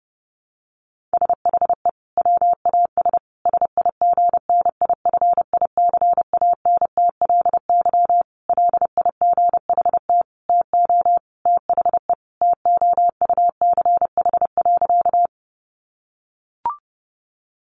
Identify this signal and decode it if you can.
Morse code, heard as a keyed tone, about 28 wpm, 700 Hz. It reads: SHE WAS SIGNIFICANTLY LIGHT TO THE TOUCH.
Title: Sentences from Top 1500 Words 30wpm